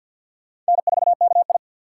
Morse code, heard as a keyed tone, about 63 wpm, 700 Hz. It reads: N4KI